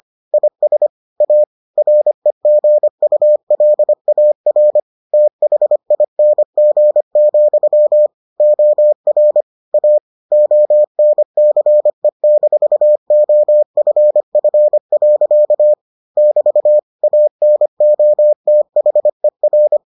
Morse code, heard as a keyed tone, about 25 wpm, 600 Hz. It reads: IS A REGULAR THING, OR A ONCE-OFF. <BT> ANOTHER